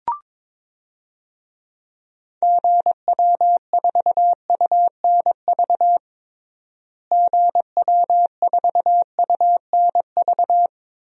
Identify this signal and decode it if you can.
Morse code, heard as a keyed tone, about 22 wpm, 700 Hz. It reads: GW4UNV GW4UNV
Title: DXCC-100 Call Signs - 6 Characters - Repeated 3x - 22wpm